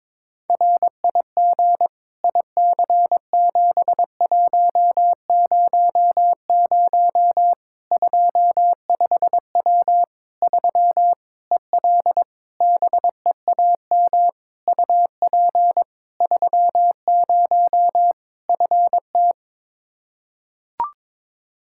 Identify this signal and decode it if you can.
Morse code, heard as a keyed tone, about 22 wpm, 700 Hz. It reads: RIG IC7100 25W 3 EL BEAM UP 30 FT
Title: Practice QSO - Exchange 2 - Info - 22wpm